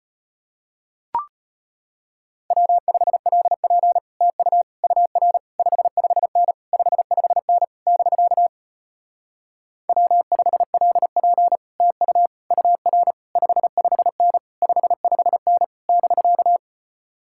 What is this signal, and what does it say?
Morse code, heard as a keyed tone, about 33 wpm, 700 Hz. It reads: W5LP TU UR 55N 55N <BK> W5LP TU UR 55N 55N <BK>